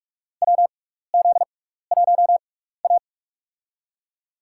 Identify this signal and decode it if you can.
Morse code, heard as a keyed tone, about 45 wpm, spaced as for 15 wpm, 700 Hz. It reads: WZ1A
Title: USA Call Signs - 6x Character Spacing - 45wpm